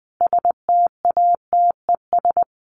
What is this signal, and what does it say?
Morse code, heard as a keyed tone, about 20 wpm, 700 Hz. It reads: STATES
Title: Sets of 5 Words Encore2 20wpm